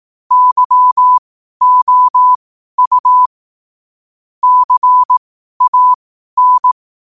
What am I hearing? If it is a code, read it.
Morse code, heard as a keyed tone, about 18 wpm, spaced as for 12 wpm, 1000 Hz. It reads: YOU CAN
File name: qotdfiles / 12wpm